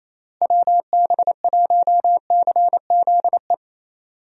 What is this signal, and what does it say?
Morse code, heard as a keyed tone, about 27 wpm, 700 Hz. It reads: WB1CZE
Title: USA Call Signs - 6 Characters - Repeated 3x - 28wpm